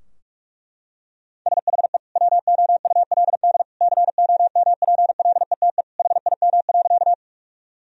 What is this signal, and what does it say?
Morse code, heard as a keyed tone, about 46 wpm, 700 Hz. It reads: SHE WOULD COMPLETE HIM.
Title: Sentences from Top 1000 Words - Review Part 2 - 45wpm